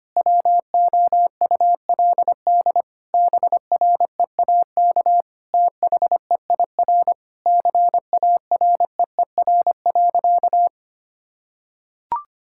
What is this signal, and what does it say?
Morse code, heard as a keyed tone, about 25 wpm, 700 Hz. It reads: WOULD BREAK THEIR CAREER.
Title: Sentences from Top 1000 Words - Review Part 6 - 25wpm